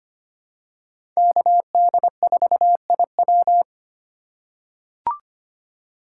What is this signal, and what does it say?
Morse code, heard as a keyed tone, about 25 wpm, 700 Hz. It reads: KD4IW
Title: USA Call Signs - 5 Characters - 4x Character Spacing - 25wpm